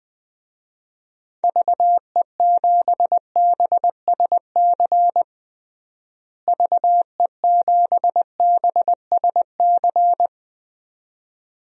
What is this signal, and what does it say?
Morse code, heard as a keyed tone, about 20 wpm, 700 Hz. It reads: VE7BSC VE7BSC
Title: DXCC-100 Call Signs - 6 Characters - Repeated 2x - 20wpm